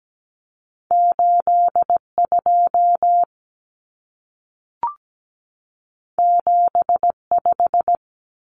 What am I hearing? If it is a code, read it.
Morse code, heard as a keyed tone, about 17 wpm, 700 Hz. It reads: 82 75
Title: Common Cut Numbers - 2-Digit 17wpm